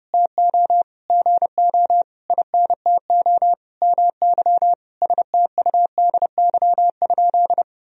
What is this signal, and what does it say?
Morse code, heard as a keyed tone, about 30 wpm, 700 Hz. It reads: TO GO INTO MY STUDY?
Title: Sentences from Top 1000 Words - Review Part 1 - 30wpm